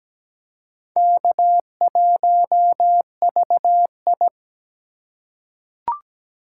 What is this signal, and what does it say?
Morse code, heard as a keyed tone, about 17 wpm, 700 Hz. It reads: K1VI
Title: USA Call Signs - 3x Character Spacing - 17wpm